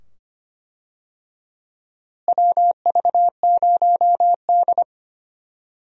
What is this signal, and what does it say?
Morse code, heard as a keyed tone, about 25 wpm, 700 Hz. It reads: WV0D
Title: USA Call Signs - 4 Characters - Repeated 2x - 25wpm